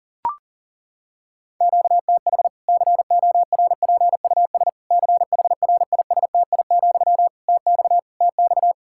Morse code, heard as a keyed tone, about 40 wpm, 700 Hz. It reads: QTH CORPUS CHRISTI, TX TX
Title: Practice QSO - Exchange 1 - Beginning - 40wpm